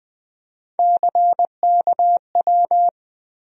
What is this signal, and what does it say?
Morse code, heard as a keyed tone, about 20 wpm, 700 Hz. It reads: CKW
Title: USA Call Sign Postfixes - 3 Letters 20wpm